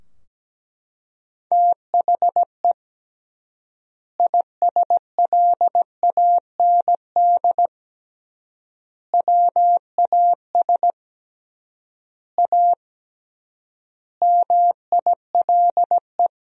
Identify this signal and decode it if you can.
Morse code, heard as a keyed tone, about 17 wpm, 700 Hz. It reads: THE ISLAND WAS A MILE
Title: 3x Word-Spacing - Sentences from Top 500 Words - 17wpm